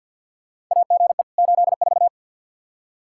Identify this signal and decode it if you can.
Morse code, heard as a keyed tone, about 47 wpm, 700 Hz. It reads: AGE 84
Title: K1USN SST QSO Practice 50wpm